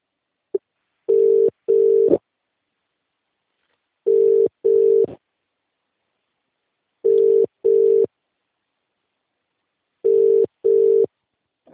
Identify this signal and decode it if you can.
Ringback tone (UK), 400 + 450 Hz.